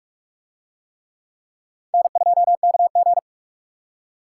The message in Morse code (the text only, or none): N2KG